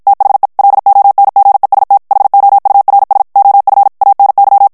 Morse code, heard as a keyed tone, about 56 wpm, 800 Hz. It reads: THE LONGEST HOURS OF ANY
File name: morse mp3 / ANGLAIS texte clair 1 vitesse 50 mots minute